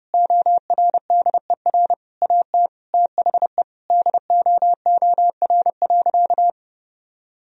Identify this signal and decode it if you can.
Morse code, heard as a keyed tone, about 30 wpm, 700 Hz. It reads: ORDER AT THE DOOR.
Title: Sentences from Top 1000 Words - Review Part 2 - 30wpm